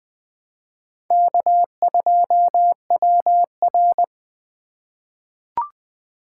Morse code, heard as a keyed tone, about 20 wpm, 700 Hz. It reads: K2WR
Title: Four Letter-Number Combinations - 20wpm